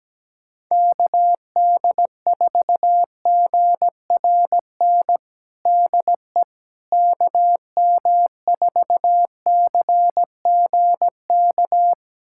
Morse code, heard as a keyed tone, about 17 wpm, 700 Hz. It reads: KD4GRN DE KM4CGK
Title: USA Callsign DE Callsign - 6 Characters - 17wpm